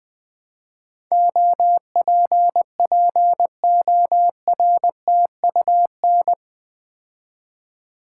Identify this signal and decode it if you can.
Morse code, heard as a keyed tone, about 20 wpm, 700 Hz. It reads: OPPORTUN
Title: Build A Word - 4 Letters At A Time - 11-Letter Words - 20wpm